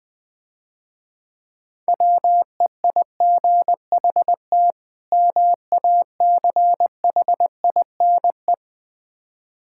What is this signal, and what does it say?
Morse code, heard as a keyed tone, about 20 wpm, 700 Hz. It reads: WEIGHT MACHINE
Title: Difficult Words - Speed-Racing with Hints - 20wpm